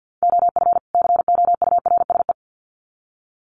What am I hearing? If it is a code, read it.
Morse code, heard as a keyed tone, about 50 wpm, 700 Hz. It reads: OF COURSE
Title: Sets of 2 Words 50wpm